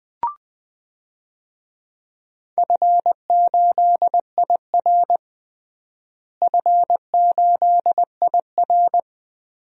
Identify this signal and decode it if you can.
Morse code, heard as a keyed tone, about 20 wpm, 700 Hz. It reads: F8IR F8IR
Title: DXCC-100 Call Signs - 4 Characters - Repeated 3x - 20wpm